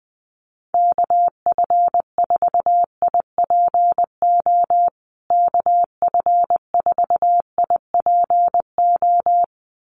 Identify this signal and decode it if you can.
Morse code, heard as a keyed tone, about 20 wpm, 700 Hz. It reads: KF4IPO KF4IPO
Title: USA Call Signs Repeated 2x 20wpm